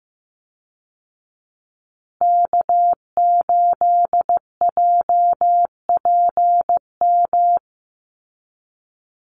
Morse code, heard as a keyed tone, about 15 wpm, 700 Hz. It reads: K8JPM
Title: Call Signs Repeated 3x - 15wpm